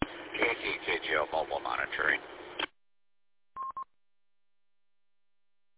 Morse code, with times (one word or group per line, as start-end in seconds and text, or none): none